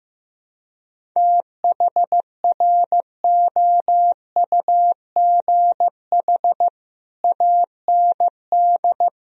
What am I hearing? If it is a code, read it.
Morse code, heard as a keyed tone, about 16 wpm, 700 Hz. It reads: THROUGH AND
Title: Sentences from Top 1000 Words - Review Part 1 - 15wpm